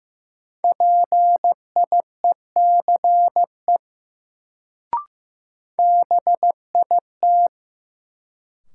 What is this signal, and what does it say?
Morse code, heard as a keyed tone, about 15 wpm, 700 Hz. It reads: PIECE BIT
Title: Sets of 3 Words - New Word Drill - 15wpm